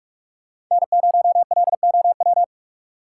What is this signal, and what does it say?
Morse code, heard as a keyed tone, about 44 wpm, 700 Hz. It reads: N0LOW
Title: USA Call Signs - 5 Characters - 6x Character Spacing - 45wpm